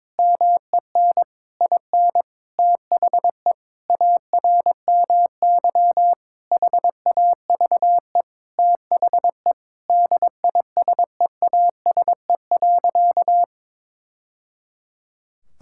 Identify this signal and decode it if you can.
Morse code, heard as a keyed tone, about 22 wpm, 700 Hz. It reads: MEN IN THE ARMY HAVE THE DISEASE.
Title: Sentences from Top 1000 Words - Review Part 6 - 22wpm